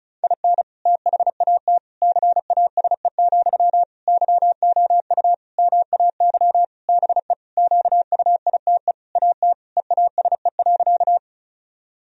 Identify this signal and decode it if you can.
Morse code, heard as a keyed tone, about 36 wpm, 700 Hz. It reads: IN THAT CASE, YOU MAY BE QUITE AT EASE.
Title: Sentences from Top 1000 Words - Review Part 1 - 35wpm